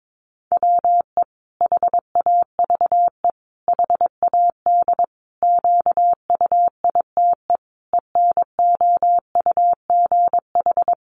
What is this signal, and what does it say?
Morse code, heard as a keyed tone, about 22 wpm, 700 Hz. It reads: WE HAVE HAD QUITE ENOUGH
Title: Sentences from Top 700 Words 22wpm